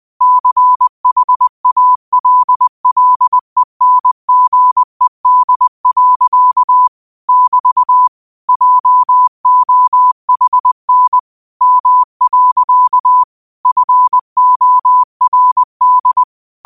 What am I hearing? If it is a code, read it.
Morse code, heard as a keyed tone, about 20 wpm, 1000 Hz. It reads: CHALLENGED. <BT> JOHN M. FORD
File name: qotdfiles / 20wpm